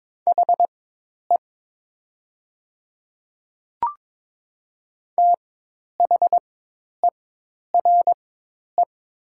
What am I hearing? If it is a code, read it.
Morse code, heard as a keyed tone, about 22 wpm, spaced as for 10 wpm, 700 Hz. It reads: HE THERE
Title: 4x Character Spacing - Top 100 Words - 22wpm